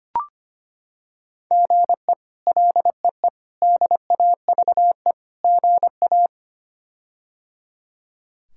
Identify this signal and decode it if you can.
Morse code, heard as a keyed tone, about 25 wpm, 700 Hz. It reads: GE LEE DAVE GA
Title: K1USN SST QSO Practice - 25wpm